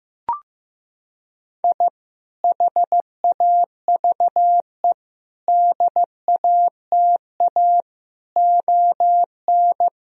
Morse code, heard as a keyed tone, about 15 wpm, 700 Hz. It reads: I HAVE DATA ON
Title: Sentences from Top 1000 Words - Review Part 3 - 15wpm